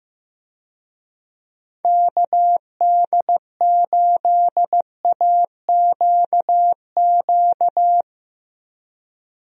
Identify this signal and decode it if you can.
Morse code, heard as a keyed tone, about 15 wpm, 700 Hz. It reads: KD8AQQ
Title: USA Call Signs - 6 Characters - 15wpm